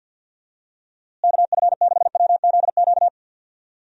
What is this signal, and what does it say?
Morse code, heard as a keyed tone, about 48 wpm, 700 Hz. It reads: KF6WZQ
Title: USA Call Signs - 6 Characters - Repeated 3x - 50wpm